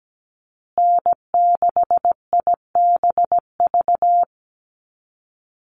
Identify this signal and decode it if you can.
Morse code, heard as a keyed tone, about 17 wpm, 700 Hz. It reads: N6IBV
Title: USA Call Signs Repeated 2x 17wpm